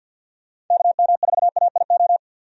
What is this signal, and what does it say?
Morse code, heard as a keyed tone, about 51 wpm, 700 Hz. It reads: KM4AIO
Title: USA Call Signs - 6 Characters - 50wpm